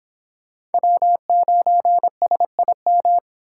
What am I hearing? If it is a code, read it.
Morse code, heard as a keyed tone, about 30 wpm, 700 Hz. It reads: W9SIM